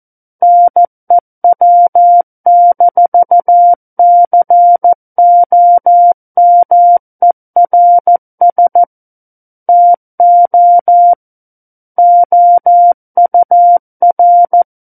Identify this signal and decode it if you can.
Morse code, heard as a keyed tone, about 14 wpm, 700 Hz. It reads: NEW-COMERS TO OUR
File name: Book One / 05-Chapter 5 - 14 WPM